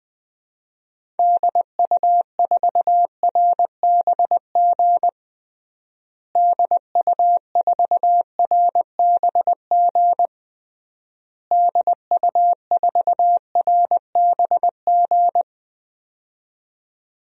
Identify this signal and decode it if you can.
Morse code, heard as a keyed tone, about 21 wpm, 700 Hz. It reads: DU4RBG DU4RBG DU4RBG
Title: DXCC-100 Call Signs - 6 Characters - Repeated 3x - 20wpm